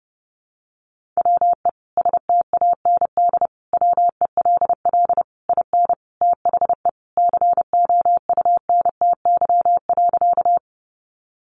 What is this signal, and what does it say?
Morse code, heard as a keyed tone, about 30 wpm, 700 Hz. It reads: WE STAND WELL IN THE COUNTY.
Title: Sentences from Top 1200 Words 30wpm